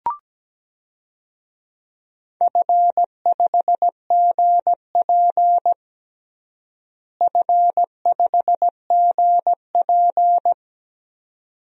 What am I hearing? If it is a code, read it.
Morse code, heard as a keyed tone, about 17 wpm, 700 Hz. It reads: F5GP F5GP
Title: DXCC-100 Call Signs - 4 Characters - Repeated 2x - 17wpm